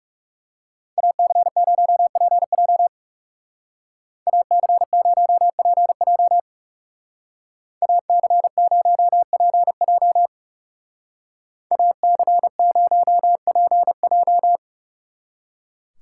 Morse code, heard as a keyed tone, about 35 wpm, 700 Hz. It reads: AC0PJ AC0PJ AC0PJ AC0PJ
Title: Speed Racing - USA Call Signs 30wpm